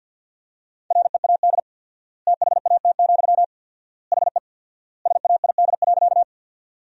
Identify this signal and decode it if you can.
Morse code, heard as a keyed tone, about 52 wpm, 700 Hz. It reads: READ THAT, HE SAID.
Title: 4x Word-Spacing - Sentences from Top 200 Words - 50wpm